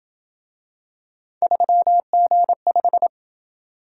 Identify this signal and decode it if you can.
Morse code, heard as a keyed tone, about 27 wpm, 700 Hz. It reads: 3G5